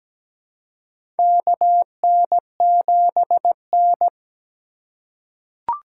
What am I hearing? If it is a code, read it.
Morse code, heard as a keyed tone, about 17 wpm, 700 Hz. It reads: KN7N
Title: USA Call Signs - 4x Character Spacing - 17wpm